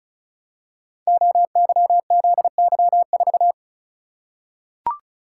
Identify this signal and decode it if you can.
Morse code, heard as a keyed tone, about 35 wpm, 700 Hz. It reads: OYZY4
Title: Code Groups - 5 Letter:Numbers - 35wpm